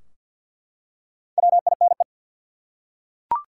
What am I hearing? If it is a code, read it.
Morse code, heard as a keyed tone, about 47 wpm, 700 Hz. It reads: WINE
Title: Four Letter Words - 50wpm